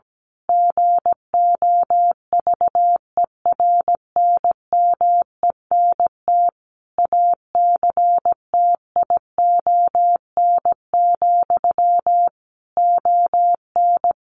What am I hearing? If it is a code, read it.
Morse code, heard as a keyed tone, about 17 wpm, 700 Hz. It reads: GOVERNMENT ACTION, ON
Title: A Sound Of Thunder 17wpm